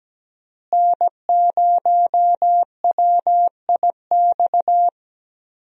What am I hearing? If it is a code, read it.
Morse code, heard as a keyed tone, about 17 wpm, 700 Hz. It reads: N0WIX
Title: USA Call Signs - 5 Characters - Repeated 3x - 17wpm